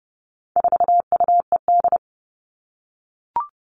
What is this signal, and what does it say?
Morse code, heard as a keyed tone, about 30 wpm, 700 Hz. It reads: 4UED
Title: All Cut Numbers - 4-Digit 30wpm